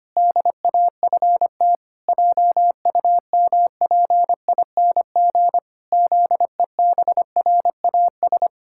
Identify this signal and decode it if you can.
Morse code, heard as a keyed tone, about 25 wpm, 700 Hz. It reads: DAFT JUMPING ZEBRAS
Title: Pangram Challenge v2 - 25wpm